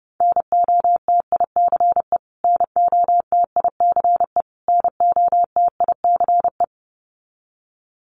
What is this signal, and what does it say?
Morse code, heard as a keyed tone, about 30 wpm, 700 Hz. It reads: NOTICE NOTICE NOTICE
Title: Top 500 Words Repeated 30wpm